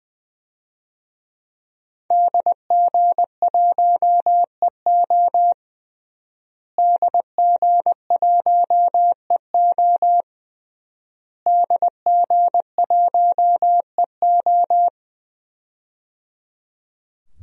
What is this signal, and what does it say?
Morse code, heard as a keyed tone, about 21 wpm, 700 Hz. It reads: DG1EO DG1EO DG1EO
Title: DXCC-100 Call Signs - 5 Characters - Repeated 3x - 20wpm